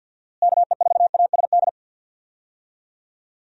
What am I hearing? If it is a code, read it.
Morse code, heard as a keyed tone, about 52 wpm, 700 Hz. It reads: KE4ASD
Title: USA Call Signs - 6 Characters - Repeated 2x - 50wpm